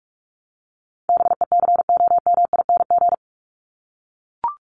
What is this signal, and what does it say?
Morse code, heard as a keyed tone, about 44 wpm, 700 Hz. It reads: BECOMING
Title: ING Suffix - Top 100 Words with Base Words 45wpm